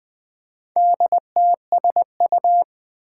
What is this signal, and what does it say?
Morse code, heard as a keyed tone, about 20 wpm, 700 Hz. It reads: DTSU
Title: Four Challenging Letter-Number Combinations 20wpm